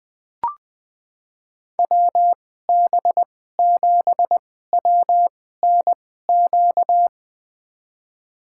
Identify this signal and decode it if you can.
Morse code, heard as a keyed tone, about 20 wpm, spaced as for 14 wpm, 700 Hz. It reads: WB7WNQ
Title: USA Call Signs - 6 Characters - 2x Character Spacing - 20wpm